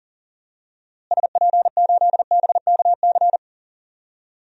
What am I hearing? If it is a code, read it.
Morse code, heard as a keyed tone, about 40 wpm, 700 Hz. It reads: SP8BKC
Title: DXCC-100 Call Signs - 6 Characters - Repeated 2x - 40wpm